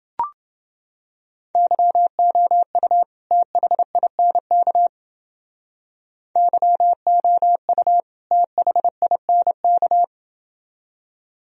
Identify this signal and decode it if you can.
Morse code, heard as a keyed tone, about 29 wpm, 700 Hz. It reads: YOU THINK YOU THINK